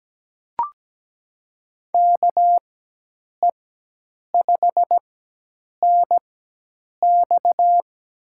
Morse code, heard as a keyed tone, about 17 wpm, spaced as for 8 wpm, 700 Hz. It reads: KE5NX
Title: USA Call Signs - 4x Character Spacing - 17wpm